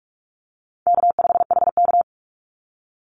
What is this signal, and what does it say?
Morse code, heard as a keyed tone, about 46 wpm, 700 Hz. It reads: K5HK
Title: Easy Call Signs - 45wpm